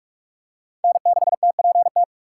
Morse code, heard as a keyed tone, about 46 wpm, 700 Hz. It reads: N6TPT
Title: USA Call Signs - 5 Characters - 3x Character Spacing - 45wpm